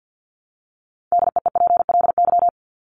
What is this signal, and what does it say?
Morse code, heard as a keyed tone, about 49 wpm, 700 Hz. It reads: DEEPLY
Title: Top 1751-2000 Words 50wpm